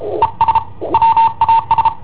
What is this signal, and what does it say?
Morse code, heard as a keyed tone, about 33 wpm, 910 Hz. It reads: ES WAS
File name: i7ohp / cassetta i7ohp B